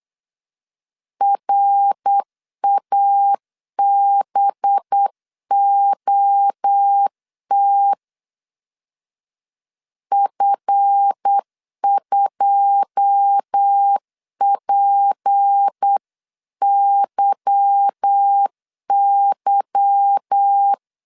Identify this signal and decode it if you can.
Morse code, heard as a keyed tone, about 8 wpm, 800 Hz. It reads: RABOT F2PYY
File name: CW / DET / 1 9 1500C S B 8wpm 5624